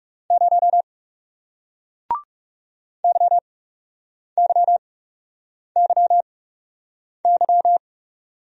Morse code, heard as a keyed tone, about 37 wpm, spaced as for 8 wpm, 700 Hz. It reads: 0 YYYY
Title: Speed-Racing Single Letter-Number 30wpm